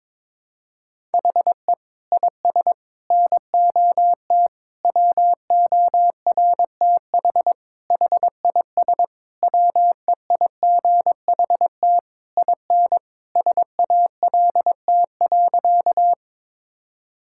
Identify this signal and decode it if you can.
Morse code, heard as a keyed tone, about 22 wpm, 700 Hz. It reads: HE IS NOT WORTH HIS WEIGHT IN SALT.
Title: Sentences from Top 1000 Words - Review Part 5 - 22wpm